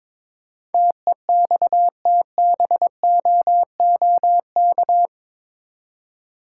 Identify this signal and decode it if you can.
Morse code, heard as a keyed tone, about 22 wpm, 700 Hz. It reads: TEXTBOOK
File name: practice / 22wpm / Eight Letter Words - 22wpm